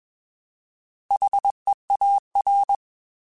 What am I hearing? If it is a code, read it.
Morse code, heard as a keyed tone, about 21 wpm, 800 Hz. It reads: HEAR
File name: Alphabetical / Alphabetical mp3s / Hs @20